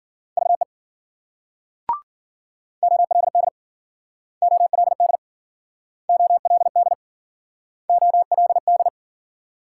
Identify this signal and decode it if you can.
Morse code, heard as a keyed tone, about 52 wpm, 700 Hz. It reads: VE OLD OLD OLD OLD